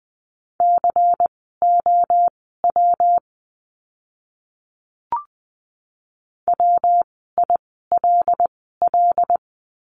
Morse code, heard as a keyed tone, about 20 wpm, spaced as for 14 wpm, 700 Hz. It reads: COW WILL
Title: 115 Lesson 19f - Letter w - Words 20wpm